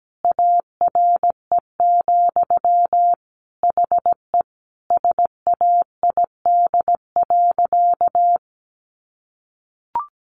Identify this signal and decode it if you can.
Morse code, heard as a keyed tone, about 17 wpm, 700 Hz. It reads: ARE, HE SAID.
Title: Sentences from Top 100 Words 17wpm